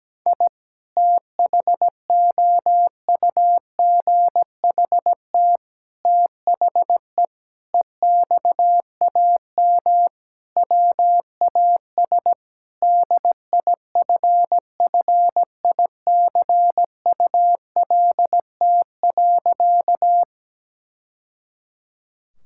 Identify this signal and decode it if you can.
Morse code, heard as a keyed tone, about 17 wpm, 700 Hz. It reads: I THOUGHT THE EXAM WAS DIFFICULT.
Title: Sentences from Top 1000 Words - Review Part 5 - 17wpm